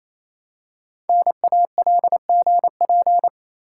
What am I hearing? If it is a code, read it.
Morse code, heard as a keyed tone, about 27 wpm, 690 Hz. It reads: NALGP
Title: Code Groups - 5 Letters - 28wpm